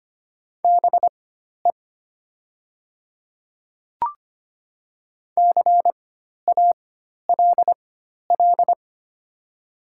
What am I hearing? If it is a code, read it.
Morse code, heard as a keyed tone, about 25 wpm, spaced as for 12 wpm, 700 Hz. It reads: BE CALL
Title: 4x Character Spacing - Top 100 Words - 25wpm